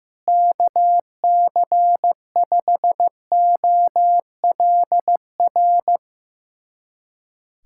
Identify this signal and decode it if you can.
Morse code, heard as a keyed tone, about 15 wpm, 700 Hz. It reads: KC5OLR
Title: USA Callsign DE Callsign - 6 Characters - 15wpm